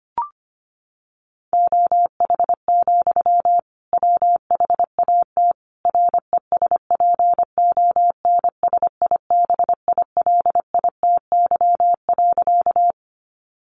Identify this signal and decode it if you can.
Morse code, heard as a keyed tone, about 25 wpm, 700 Hz. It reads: OH, WHAT RESPONSIBILITY.
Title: Sentences from Top 1200 Words 25wpm